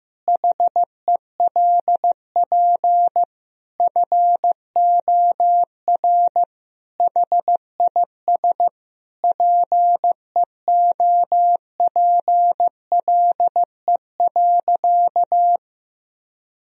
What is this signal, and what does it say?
Morse code, heard as a keyed tone, about 15 wpm, 700 Hz. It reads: HELP FOR HIS PEOPLE.
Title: Sentences from Top 1000 Words - Review Part 3 - 15wpm